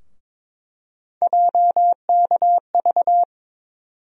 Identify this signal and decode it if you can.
Morse code, heard as a keyed tone, about 22 wpm, 700 Hz. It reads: JKV
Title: USA Call Sign Postfixes - 3 Letters 22wpm